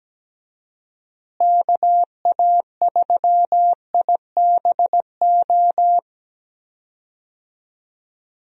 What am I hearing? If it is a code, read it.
Morse code, heard as a keyed tone, about 17 wpm, 700 Hz. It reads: KA3IBO
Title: USA Call Signs - 6 Characters - Repeated 2x - 17wpm